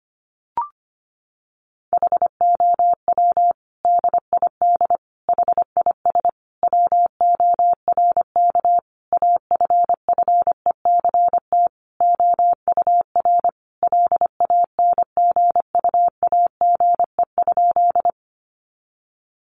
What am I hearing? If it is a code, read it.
Morse code, heard as a keyed tone, about 25 wpm, 700 Hz. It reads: HOW DID HIS WORK AFFECT OUR LANGUAGE?
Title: Sentences from Top 900 Words 25wpm